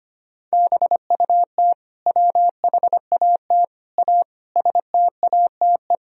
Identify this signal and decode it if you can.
Morse code, heard as a keyed tone, about 25 wpm, 700 Hz. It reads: BUT WHAT A STATE
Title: Sentences from Top 1000 Words - Review Part 2 - 25wpm